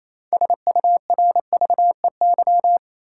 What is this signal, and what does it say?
Morse code, heard as a keyed tone, about 27 wpm, 700 Hz. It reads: SURVEY
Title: Six Letter Words - 28wpm